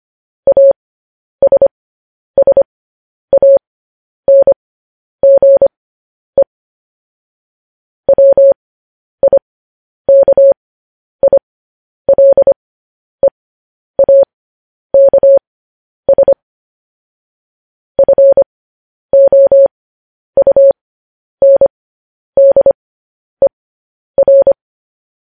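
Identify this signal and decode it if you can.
Morse code, heard as a keyed tone, about 25 wpm, spaced as for 10 wpm, 550 Hz. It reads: ASSANGE WIKILEAKS FOUNDER
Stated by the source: Morse code news headlines.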